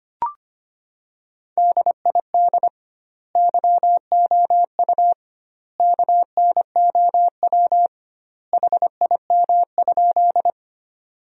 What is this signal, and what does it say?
Morse code, heard as a keyed tone, about 25 wpm, 700 Hz. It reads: DID YOU KNOW HIM?
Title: 2x Word-Spacing - Sentences from Top 100 Words - 25wpm